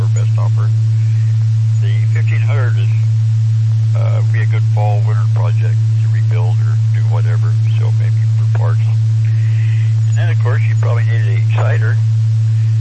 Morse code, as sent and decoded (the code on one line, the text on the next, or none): none